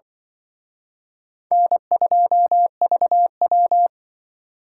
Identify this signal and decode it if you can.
Morse code, heard as a keyed tone, about 24 wpm, 700 Hz. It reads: N2VW
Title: Speed Racing - USA 1x2 Call Signs 20wpm